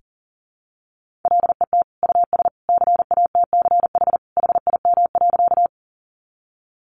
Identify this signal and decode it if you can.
Morse code, heard as a keyed tone, about 40 wpm, 700 Hz. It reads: LET US CATCH HIM.
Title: Sentences from Top 1500 Words 40wpm